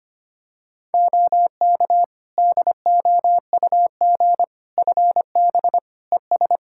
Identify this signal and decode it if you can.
Morse code, heard as a keyed tone, about 25 wpm, 700 Hz. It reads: OK DOUG FB ES
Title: Practice QSO - Exchange 2 - Info - 25wpm